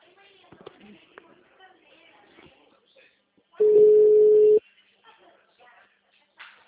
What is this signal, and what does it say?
Ringback tone (Europe, 425 Hz), with speech.